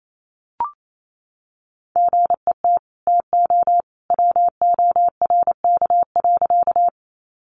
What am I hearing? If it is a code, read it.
Morse code, heard as a keyed tone, about 28 wpm, 700 Hz. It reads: GET TO WORK.
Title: Sentences from Top 200 Words 28wpm